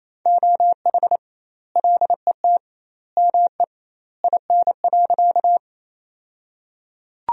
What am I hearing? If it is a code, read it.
Morse code, heard as a keyed tone, about 28 wpm, 700 Hz. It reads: OH LET ME IN.
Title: 2x Word-Spacing - Sentences from Top 500 Words - 28wpm